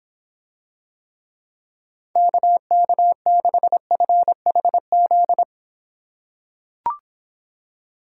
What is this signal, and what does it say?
Morse code, heard as a keyed tone, about 26 wpm, 700 Hz. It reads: KK6FHZ